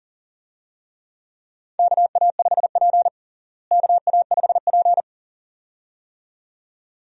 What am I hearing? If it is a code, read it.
Morse code, heard as a keyed tone, about 40 wpm, 700 Hz. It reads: KA5P KA5P
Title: USA Call Signs - 4 Characters - Repeated 2x - 40wpm